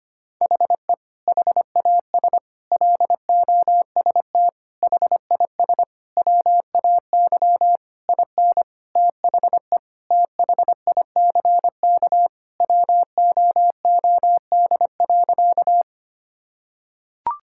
Morse code, heard as a keyed tone, about 25 wpm, 700 Hz. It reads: HE HAS LOST HIS WAY IN THE THICK WOOD.
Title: Sentences from Top 1600 Words 25wpm